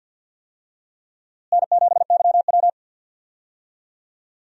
Morse code, heard as a keyed tone, about 51 wpm, 700 Hz. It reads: N7YW
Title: USA Call Signs - 4 Characters - Repeated 3x - 50wpm